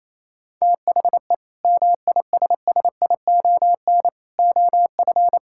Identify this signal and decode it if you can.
Morse code, heard as a keyed tone, about 28 wpm, 700 Hz. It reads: THE MISSION OF
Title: Sentences from Top 1600 Words 28wpm